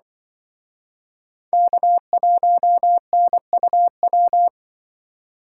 Morse code, heard as a keyed tone, about 24 wpm, 700 Hz. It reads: K1NUW
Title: Speed Racing - USA 1x3 Call Signs 20wpm